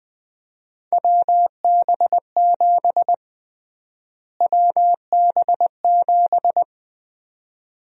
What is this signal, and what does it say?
Morse code, heard as a keyed tone, about 20 wpm, 700 Hz. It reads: WB7 WB7
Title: USA Call Sign Prefixes - Repeated 2x - 20wpm